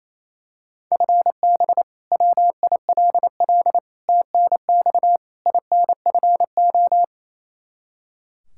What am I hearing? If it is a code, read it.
Morse code, heard as a keyed tone, about 28 wpm, 700 Hz. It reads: FB WILL TNX INFO
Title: Practice QSO - Exchange 3 - More Info - 28wpm